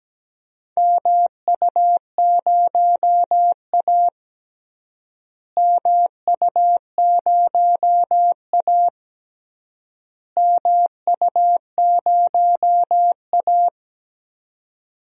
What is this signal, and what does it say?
Morse code, heard as a keyed tone, about 17 wpm, 700 Hz. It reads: MU0A MU0A MU0A
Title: DXCC-100 Call Signs - 4 Characters - Repeated 3x - 17wpm